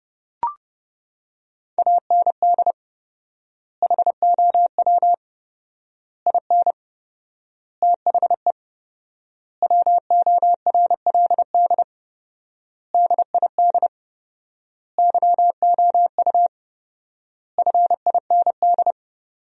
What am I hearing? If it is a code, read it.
Morse code, heard as a keyed tone, about 32 wpm, 700 Hz. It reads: AND HOW IN THE WORLD DID YOU FIND
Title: 4x Word-Spacing - Sentences from Top 200 Words - 30wpm